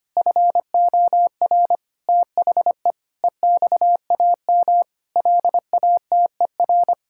Morse code, heard as a keyed tone, about 25 wpm, 700 Hz. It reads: FOR THE EXAM LATER
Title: Sentences from Top 1000 Words - Review Part 5 - 25wpm